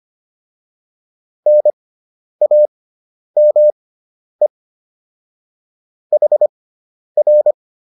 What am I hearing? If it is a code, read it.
Morse code, heard as a keyed tone, about 25 wpm, spaced as for 10 wpm, 600 Hz. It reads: NAME HR